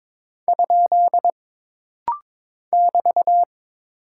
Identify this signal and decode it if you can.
Morse code, heard as a keyed tone, about 22 wpm, 700 Hz. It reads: ? <BT>
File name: practice / 22wpm / Single Punctuation-Prosign - Rapid-Fire - 22wpm